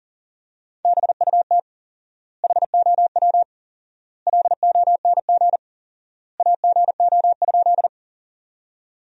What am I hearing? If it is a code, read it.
Morse code, heard as a keyed tone, about 40 wpm, 700 Hz. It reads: BUT HOW LONG AGO?
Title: 4x Word-Spacing - Sentences from Top 500 Words - 40wpm